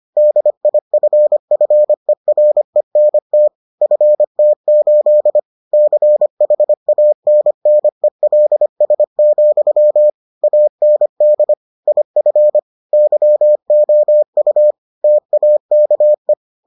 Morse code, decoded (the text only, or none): DIFFERENT FT8 CHANNELS, AND IF YOU TAKE